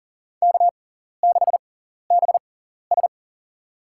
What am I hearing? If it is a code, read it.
Morse code, heard as a keyed tone, about 40 wpm, spaced as for 14 wpm, 700 Hz. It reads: K6BS
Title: USA Call Signs - 4 Characters - 6x Character Spacing - 40wpm